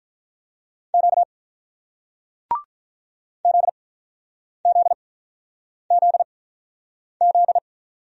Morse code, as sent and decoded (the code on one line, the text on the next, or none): --.- / --.. --.. --.. --..
Q ZZZZ